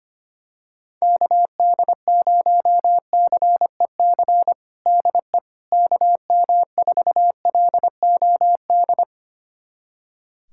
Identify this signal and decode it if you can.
Morse code, heard as a keyed tone, about 25 wpm, 700 Hz. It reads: KD0CEC DE KM4LOD
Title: USA Callsign DE Callsign - 6 Characters - 25wpm